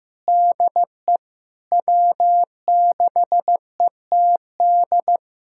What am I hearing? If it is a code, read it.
Morse code, heard as a keyed tone, about 15 wpm, 700 Hz. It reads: DE W6ETD
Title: USA Callsign DE Callsign - 5 Characters - 15wpm